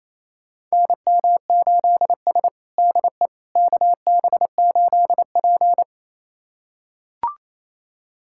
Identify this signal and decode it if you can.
Morse code, heard as a keyed tone, about 28 wpm, 700 Hz. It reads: NM8S DE KB8P
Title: USA Callsign DE Callsign - 4 Characters - 28wpm